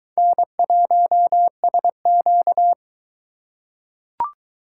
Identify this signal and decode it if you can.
Morse code, heard as a keyed tone, about 22 wpm, 700 Hz. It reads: N1SQ